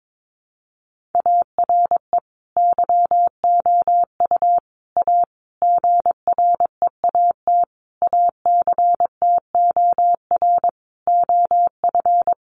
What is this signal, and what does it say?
Morse code, heard as a keyed tone, about 22 wpm, 700 Hz. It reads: ARE YOU A GREAT ACTOR OF
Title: Sentences from Top 1500 Words 22wpm